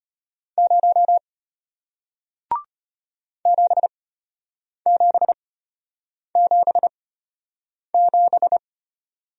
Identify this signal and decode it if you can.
Morse code, heard as a keyed tone, about 33 wpm, spaced as for 8 wpm, 700 Hz. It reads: 0 7777